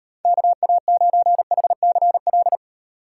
Morse code, heard as a keyed tone, about 38 wpm, 700 Hz. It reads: KA9HCL